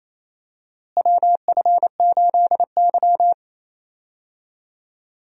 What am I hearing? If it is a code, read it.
Morse code, heard as a keyed tone, about 28 wpm, 700 Hz. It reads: WF8Y
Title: USA Call Signs - 4 Characters - Repeated 2x - 28wpm